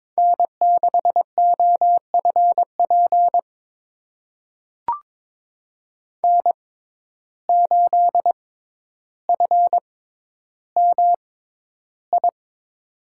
Morse code, heard as a keyed tone, about 22 wpm, spaced as for 11 wpm, 700 Hz. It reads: N6OFP N8FMI
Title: USA Call Signs - 5 Characters - 6x Character Spacing - 22wpm